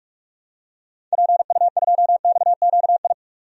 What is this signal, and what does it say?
Morse code, heard as a keyed tone, about 42 wpm, 690 Hz. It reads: PU2XQI
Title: DXCC-100 Call Signs - 6 Characters - 45wpm